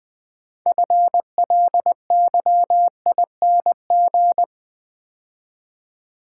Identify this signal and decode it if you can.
Morse code, heard as a keyed tone, about 20 wpm, 700 Hz. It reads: FLYING
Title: Build A Word - 3 Letters At A Time - 6-Letter Words - 20wpm